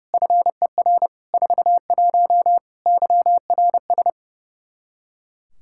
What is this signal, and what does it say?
Morse code, heard as a keyed tone, about 32 wpm, 700 Hz. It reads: FER 41 YRS
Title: Practice QSO - Exchange 3 - More Info - 30wpm